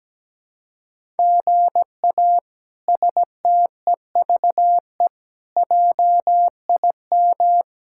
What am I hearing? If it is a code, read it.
Morse code, heard as a keyed tone, about 17 wpm, 700 Hz. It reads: GA STEVE JIM
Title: K1USN SST QSO Practice - 17wpm